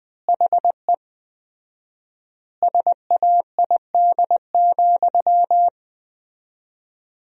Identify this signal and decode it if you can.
Morse code, heard as a keyed tone, about 19 wpm, 700 Hz. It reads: HE SAID,
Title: 4x Word-Spacing - Sentences from Top 500 Words - 20wpm